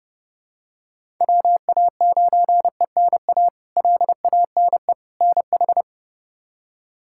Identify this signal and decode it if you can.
Morse code, heard as a keyed tone, about 32 wpm, 700 Hz. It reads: WA9ENA LANE NH
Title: K1USN SST QSO Practice - 30wpm